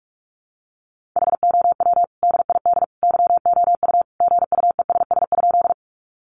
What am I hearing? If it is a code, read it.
Morse code, heard as a keyed tone, about 44 wpm, 700 Hz. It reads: HOW DID YOU GUESS?
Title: Sentences from Top 1500 Words 45wpm